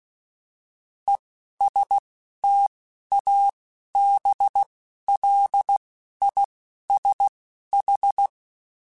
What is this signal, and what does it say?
Morse code, heard as a keyed tone, about 16 wpm, spaced as for 11 wpm, 800 Hz. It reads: ESTABLISH
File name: Alphabetical / Alphabetical mp3s / Es @10